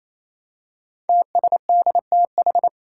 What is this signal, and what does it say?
Morse code, heard as a keyed tone, about 27 wpm, 700 Hz. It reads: TSDTH
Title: Five Challenging Letter-Number Combinations 28wpm